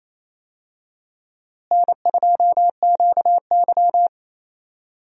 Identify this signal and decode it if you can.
Morse code, heard as a keyed tone, about 28 wpm, 700 Hz. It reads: N2QY
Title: USA Call Signs - 4 Characters - Repeated 2x - 28wpm